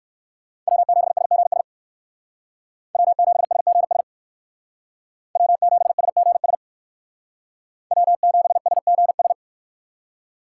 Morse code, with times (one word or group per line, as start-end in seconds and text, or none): 0.7-1.6 W7SGS
2.9-4.0 W7SGS
5.4-6.5 W7SGS
7.9-9.3 W7SGS